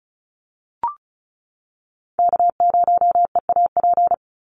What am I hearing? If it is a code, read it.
Morse code, heard as a keyed tone, about 34 wpm, 700 Hz. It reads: K0EAP
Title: NA SOTA Chaser Callsigns 35wpm